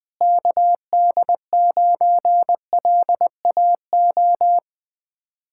Morse code, heard as a keyed tone, about 20 wpm, 700 Hz. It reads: KD9LAO